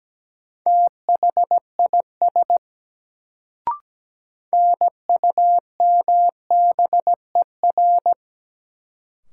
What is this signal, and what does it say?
Morse code, heard as a keyed tone, about 17 wpm, 700 Hz. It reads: THIS NUMBER
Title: Sets of 2 Words - New Word Drill - 17wpm